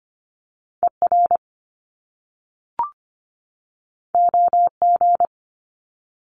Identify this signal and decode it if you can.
Morse code, heard as a keyed tone, about 25 wpm, 700 Hz. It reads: ER OG
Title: Top Two Letter Combinations in Words 25wpm